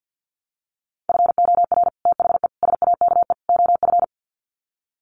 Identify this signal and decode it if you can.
Morse code, heard as a keyed tone, about 50 wpm, 700 Hz. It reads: FOR THE SAKE OF
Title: Sets of 4 Words 50wpm